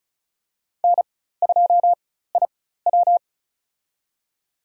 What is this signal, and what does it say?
Morse code, heard as a keyed tone, about 34 wpm, spaced as for 16 wpm, 700 Hz. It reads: N2IW
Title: USA Call Signs - 4 Characters - 4x Character Spacing - 35wpm